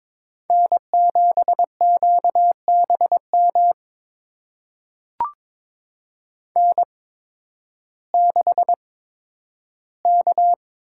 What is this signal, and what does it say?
Morse code, heard as a keyed tone, about 22 wpm, spaced as for 12 wpm, 700 Hz. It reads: N7QBM N6K
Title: USA Call Signs - 5 Characters - 8x Character Spacing - 22wpm